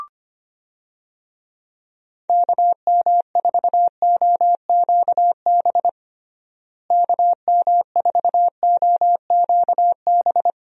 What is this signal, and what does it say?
Morse code, heard as a keyed tone, about 25 wpm, 700 Hz. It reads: KM4OQB KM4OQB
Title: USA Call Signs - 6 Characters - Repeated 2x - 25wpm